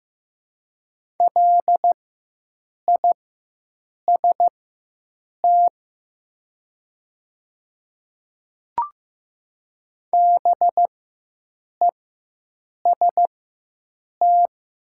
Morse code, heard as a keyed tone, about 15 wpm, spaced as for 7 wpm, 700 Hz. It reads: LIST BEST
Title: 4x Character Spacing - Top 500 Words - 15wpm